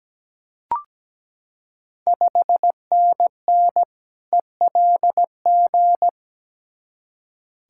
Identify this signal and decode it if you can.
Morse code, heard as a keyed tone, about 17 wpm, 700 Hz. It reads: 5NN ELG
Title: Ontario QSO Party Exchange 17wpm